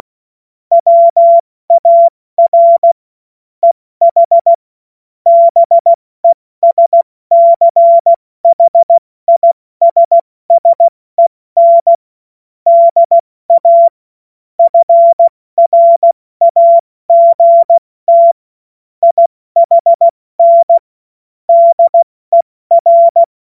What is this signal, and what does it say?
Morse code, heard as a keyed tone, about 16 wpm, spaced as for 14 wpm, 680 Hz. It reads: WAR EH BESCHISSEN DA FRAGT IHN DER
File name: Morsekurs / Witze / Witz 22. Februar 2022 70 F80